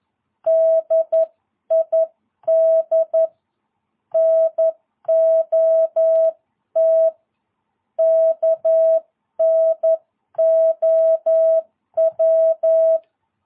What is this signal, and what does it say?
Morse code, heard as a keyed tone, about 11 wpm, 650 Hz. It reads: DID NOT KNOW
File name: jh8cbh / 2022cw